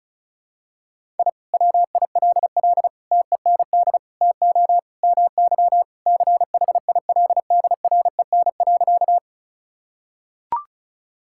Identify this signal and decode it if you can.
Morse code, heard as a keyed tone, about 34 wpm, 700 Hz. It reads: I WILL TEND TO MY CHILDREN.
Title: Sentences from Top 1000 Words - Review Part 5 - 35wpm